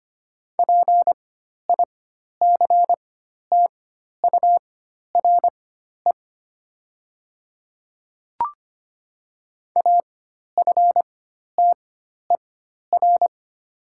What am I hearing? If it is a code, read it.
Morse code, heard as a keyed tone, about 25 wpm, spaced as for 12 wpm, 700 Hz. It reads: PICTURE AFTER
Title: 4x Character Spacing - Top 200 Words - 25wpm